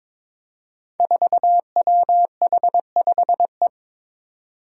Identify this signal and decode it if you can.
Morse code, heard as a keyed tone, about 22 wpm, 700 Hz. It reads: 4WH5E
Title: Five Challenging Letter-Number Combinations 22wpm